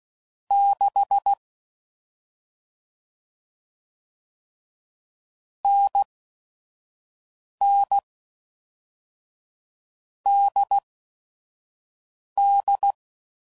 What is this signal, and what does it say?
Morse code, heard as a keyed tone, about 16 wpm, spaced as for 5 wpm, 800 Hz. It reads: 6 NN DD
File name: Combinations / Difficult Combos / NDB NDB6